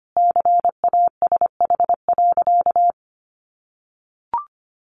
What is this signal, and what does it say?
Morse code, heard as a keyed tone, about 25 wpm, 700 Hz. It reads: CASH.
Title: Sentences from Top 1300 Words 25wpm